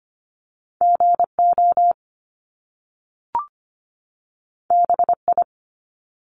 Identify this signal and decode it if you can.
Morse code, heard as a keyed tone, about 25 wpm, 700 Hz. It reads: GO BI
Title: Top Two Letter Combinations in Words 25wpm